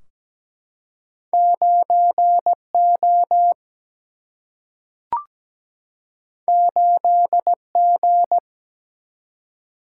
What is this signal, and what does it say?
Morse code, heard as a keyed tone, about 17 wpm, 700 Hz. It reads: 9 O 8 G
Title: Single-Letter Single-Number 17wpm